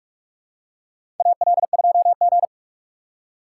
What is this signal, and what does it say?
Morse code, heard as a keyed tone, about 46 wpm, 700 Hz. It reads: AL2G
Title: USA Call Signs - 4 Characters - 6x Character Spacing - 45wpm